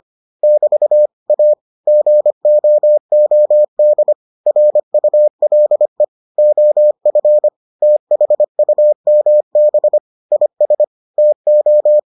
Morse code, heard as a keyed tone, about 25 wpm, 600 Hz. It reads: <BT> A GOOD RULE OF THUMB IS TO